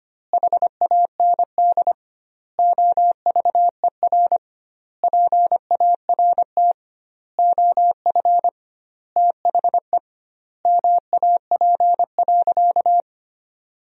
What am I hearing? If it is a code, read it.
Morse code, heard as a keyed tone, about 25 wpm, 700 Hz. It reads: HAND OVER PART OF THE MAP.
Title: 2x Word-Spacing - Sentences from Top 400 Words - 25wpm